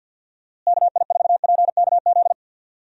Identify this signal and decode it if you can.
Morse code, heard as a keyed tone, about 52 wpm, 700 Hz. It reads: KI4PKZ
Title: USA Call Signs - 6 Characters - Repeated 3x - 50wpm